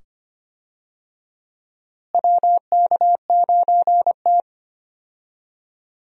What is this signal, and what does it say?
Morse code, heard as a keyed tone, about 25 wpm, 700 Hz. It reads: WK9T
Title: USA Call Signs - 4 Characters - Repeated 2x - 25wpm